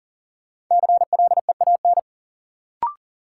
Morse code, heard as a keyed tone, about 40 wpm, 700 Hz. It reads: CLEAN
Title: Sentences from Top 1100 Words - New Word Drill - 40wpm